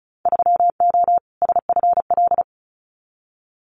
Morse code, heard as a keyed tone, about 34 wpm, 700 Hz. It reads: 3O SFL
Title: Winter Field Day Exchanges 35wpm